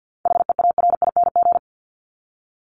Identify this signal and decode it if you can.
Morse code, heard as a keyed tone, about 50 wpm, 700 Hz. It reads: HEARING
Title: ING Suffix - Top 100 Words with Base Words 50wpm